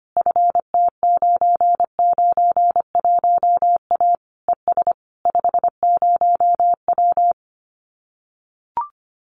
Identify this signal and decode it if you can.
Morse code, heard as a keyed tone, about 25 wpm, 700 Hz. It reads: FT991A ES 50W
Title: Common QSO Phrases Encore 25wpm